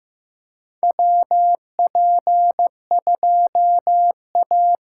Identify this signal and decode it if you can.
Morse code, heard as a keyed tone, about 15 wpm, 700 Hz. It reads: WP2A
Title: USA Call Signs - 6 Characters - 6x Character Spacing - 15wpm